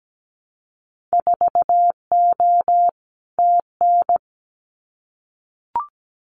Morse code, heard as a keyed tone, about 17 wpm, 700 Hz. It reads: 4O TN
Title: Winter Field Day Exchanges Encore 17wpm